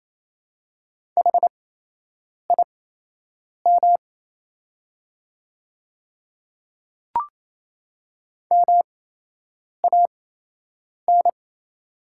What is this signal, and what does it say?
Morse code, heard as a keyed tone, about 28 wpm, spaced as for 8 wpm, 700 Hz. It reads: HIM MAN
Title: 8x Character Spacing - Top 300 Words - 28wpm